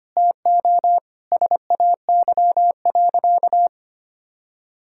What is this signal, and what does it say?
Morse code, heard as a keyed tone, about 25 wpm, 700 Hz. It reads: TO SAY.
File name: practice / 25wpm / Sentences with 4-Word Phrases - 25wpm